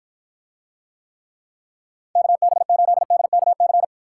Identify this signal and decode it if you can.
Morse code, heard as a keyed tone, about 54 wpm, 700 Hz. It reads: KB8DKC